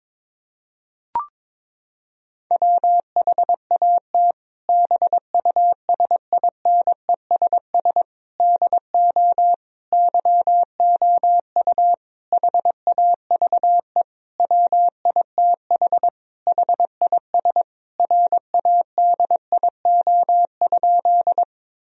Morse code, heard as a keyed tone, about 22 wpm, 700 Hz. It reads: WHAT BUSINESS DO YOU HAVE WITH HIS RADIO?
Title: Sentences from Top 1000 Words - Review Part 3 - 22wpm